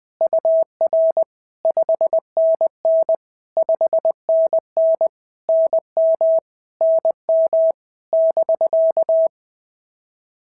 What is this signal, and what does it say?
Morse code, heard as a keyed tone, about 20 wpm, 650 Hz. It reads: UR 5NN 5NN NM NM <BK>
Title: POTA QSO Practice - 20wpm